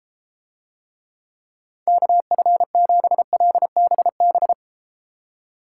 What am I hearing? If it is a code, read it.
Morse code, heard as a keyed tone, about 32 wpm, 700 Hz. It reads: KF7LBB